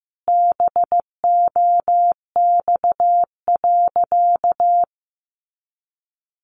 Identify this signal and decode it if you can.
Morse code, heard as a keyed tone, about 15 wpm, 700 Hz. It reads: BOX.
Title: Sentences from Top 500 Words 15wpm